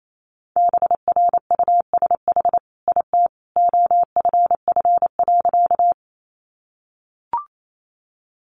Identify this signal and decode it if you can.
Morse code, heard as a keyed tone, about 28 wpm, 700 Hz. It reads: BRUSH IT OFF.
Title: Sentences from Top 1100 Words 28wpm